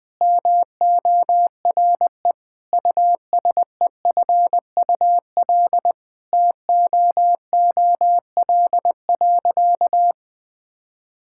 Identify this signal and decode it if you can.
Morse code, heard as a keyed tone, about 20 wpm, 700 Hz. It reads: MORE USEFUL TOOL.
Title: Sentences from Top 1000 Words - Review Part 6 - 20wpm